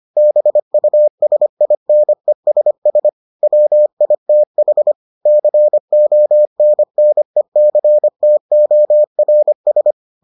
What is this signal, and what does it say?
Morse code, heard as a keyed tone, about 25 wpm, 600 Hz. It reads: BUSINESS WITH CONNECTORS